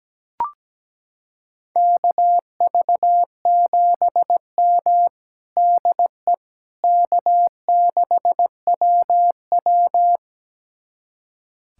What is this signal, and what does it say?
Morse code, heard as a keyed tone, about 17 wpm, 700 Hz. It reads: KV7M DE K6WW
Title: USA Callsign DE Callsign - 4 Characters - 17wpm